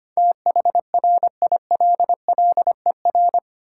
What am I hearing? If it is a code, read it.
Morse code, heard as a keyed tone, about 25 wpm, 700 Hz. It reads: THRILLER
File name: practice / 25wpm / Difficult Words with Hints - 25wpm